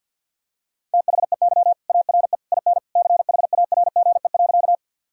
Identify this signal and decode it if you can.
Morse code, heard as a keyed tone, about 50 wpm, 700 Hz. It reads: THEY ARE IN CHARGE.
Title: Sentences from Top 1000 Words - Review Part 6 - 50wpm